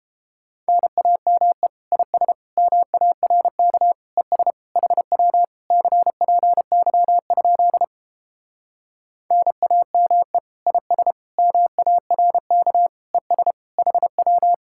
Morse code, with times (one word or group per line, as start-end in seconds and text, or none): 0.7-1.7 NAME
1.9-2.3 IS
2.6-3.9 MARK
4.2-4.5 ES
4.8-5.4 HW
5.7-7.8 CPY?
9.3-10.4 NAME
10.7-11.1 IS
11.4-12.9 MARK
13.1-13.5 ES
13.8-14.5 HW